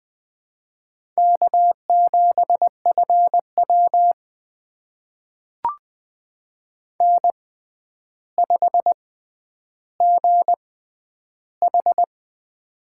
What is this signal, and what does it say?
Morse code, heard as a keyed tone, about 20 wpm, spaced as for 10 wpm, 700 Hz. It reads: K7FW N5GH
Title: USA Call Signs - 4 Characters - 6x Character Spacing - 20wpm